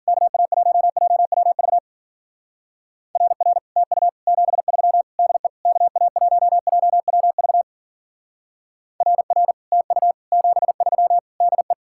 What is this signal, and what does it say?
Morse code, heard as a keyed tone, about 49 wpm, 700 Hz. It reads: KA1JWV RR TU 73 DE KA1JWV RR TU 73 DE
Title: Speed-Racing Common QSO Phrases - 40wpm